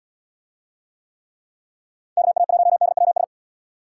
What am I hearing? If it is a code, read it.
Morse code, heard as a keyed tone, about 78 wpm, 700 Hz. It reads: KA1DWS